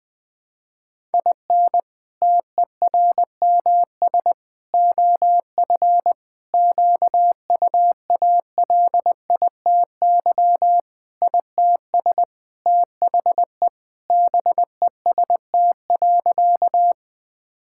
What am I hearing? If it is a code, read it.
Morse code, heard as a keyed tone, about 20 wpm, 700 Hz. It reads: IN TERMS OF QUALITY ITS THE BEST.
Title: Sentences with 3-Word Phrases - Spoken Beginning - 20wpm